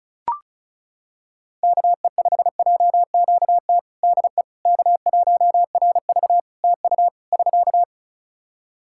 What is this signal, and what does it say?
Morse code, heard as a keyed tone, about 34 wpm, 700 Hz. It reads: KE5JQT DE K1RV TU <SK>
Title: Practice QSO - Exchange 4 - Ending - 35wpm